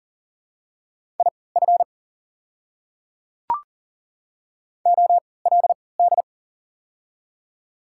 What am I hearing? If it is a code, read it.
Morse code, heard as a keyed tone, about 40 wpm, spaced as for 23 wpm, 700 Hz. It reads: IF OLD
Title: 3x Character Spacing - Top 200 Words - 40wpm